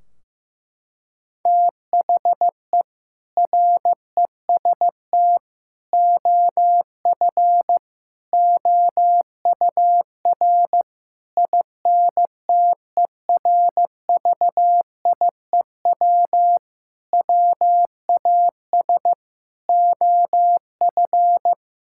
Morse code, heard as a keyed tone, about 15 wpm, 700 Hz. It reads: THE REST OF OUR INTERVIEW WAS OF
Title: Sentences from Top 1600 Words 15wpm